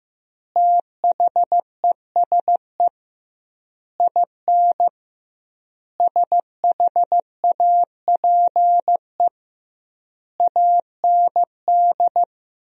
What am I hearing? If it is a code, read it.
Morse code, heard as a keyed tone, about 15 wpm, 700 Hz. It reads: THESE IN SHAPE AND
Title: 2x Word-Spacing - Sentences from Top 500 Words - 15wpm